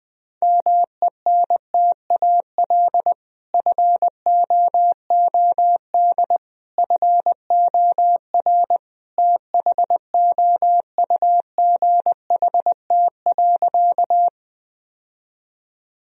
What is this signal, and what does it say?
Morse code, heard as a keyed tone, about 20 wpm, 700 Hz. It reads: MENTAL FOOD FOR THOUGHT.
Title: Sentences from Top 1000 Words - Review Part 4 - 20wpm